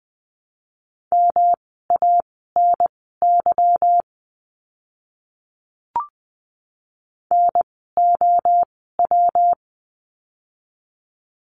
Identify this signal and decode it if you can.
Morse code, heard as a keyed tone, about 20 wpm, spaced as for 14 wpm, 700 Hz. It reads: MANY NOW
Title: 230 Lesson 37f - Letter x - Words 20wpm